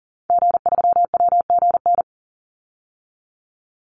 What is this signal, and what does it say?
Morse code, heard as a keyed tone, about 40 wpm, 700 Hz. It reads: G3WGN
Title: Call Signs - 40wpm